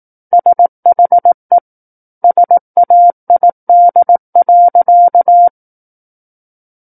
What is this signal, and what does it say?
Morse code, heard as a keyed tone, about 18 wpm, 700 Hz. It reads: SHE SAID.
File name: 18 WPM